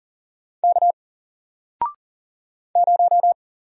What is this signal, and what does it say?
Morse code, heard as a keyed tone, about 40 wpm, 700 Hz. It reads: K 0
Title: Single Letter-Number 40wpm